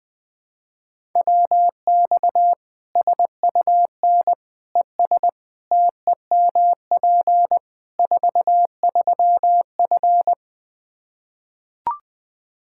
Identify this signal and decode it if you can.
Morse code, heard as a keyed tone, about 21 wpm, 700 Hz. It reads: WX SUN ES TEMP 43F
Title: Practice QSO - Exchange 2 - Info - 20wpm